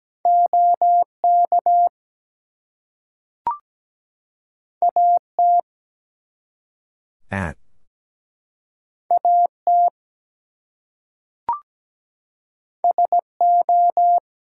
Morse code, heard as a keyed tone, about 17 wpm, 700 Hz. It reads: OK AT AT SO
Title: Two Letter Words - 17wpm